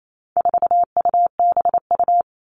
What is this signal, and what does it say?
Morse code, heard as a keyed tone, about 28 wpm, 700 Hz. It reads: 4UBU
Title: All Cut Numbers - 4-Digit 28wpm